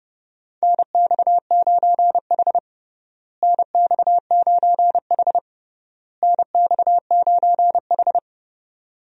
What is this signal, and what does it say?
Morse code, heard as a keyed tone, about 29 wpm, 700 Hz. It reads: NX9H NX9H NX9H
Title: USA Call Signs - 4 Characters - Repeated 3x - 30wpm